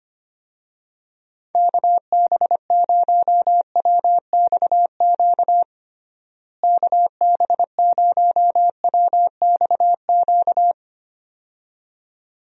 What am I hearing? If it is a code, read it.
Morse code, heard as a keyed tone, about 25 wpm, 700 Hz. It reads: KB0WXQ KB0WXQ
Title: USA Call Signs - 6 Characters - Repeated 2x - 25wpm